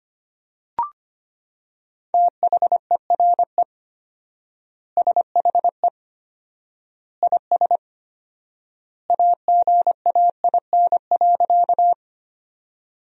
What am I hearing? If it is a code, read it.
Morse code, heard as a keyed tone, about 25 wpm, 700 Hz. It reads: THERE SHE IS AGAIN.
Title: 4x Word-Spacing - Sentences from Top 200 Words - 25wpm